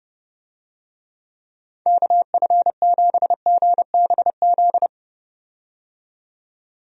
Morse code, heard as a keyed tone, about 32 wpm, 700 Hz. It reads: KF7GBZ
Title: USA Call Signs - 6 Characters - 30wpm